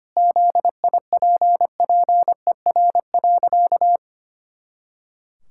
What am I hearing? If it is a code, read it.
Morse code, heard as a keyed tone, about 25 wpm, 700 Hz. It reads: ZIPPER.
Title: Pangram Challenge v2 - 25wpm